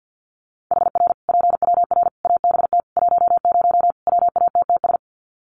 Morse code, heard as a keyed tone, about 50 wpm, 700 Hz. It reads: HR PWR ABT 10 WATTS
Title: Common QSO Phrases 50wpm